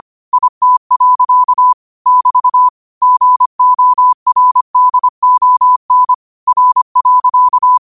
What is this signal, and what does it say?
Morse code, heard as a keyed tone, about 25 wpm, 1000 Hz. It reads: IT. <BT> GORDON R.
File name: qotdfiles / 25wpm